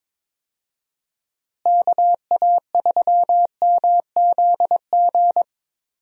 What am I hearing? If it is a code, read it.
Morse code, heard as a keyed tone, about 22 wpm, 700 Hz. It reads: KA3MZG
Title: USA Call Signs - 6 Characters - Repeated 3x - 22wpm